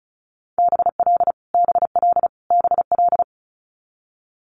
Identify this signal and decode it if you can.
Morse code, heard as a keyed tone, about 36 wpm, 700 Hz. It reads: BL BL BL
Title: Multi-letter Phonemes Repeated 3x 35wpm